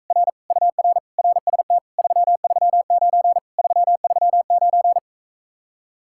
Morse code, heard as a keyed tone, about 45 wpm, 700 Hz. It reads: R UR RST 339 339